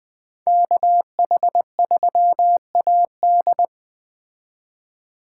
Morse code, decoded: KH3AD

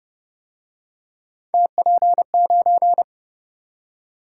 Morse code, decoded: TP9

